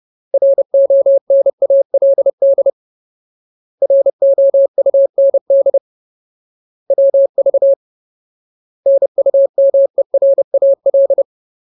Morse code, heard as a keyed tone, about 30 wpm, 550 Hz. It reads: RONALD ROUND WV NUMERAL
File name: Ultimate Word List / 30WPM Ultimate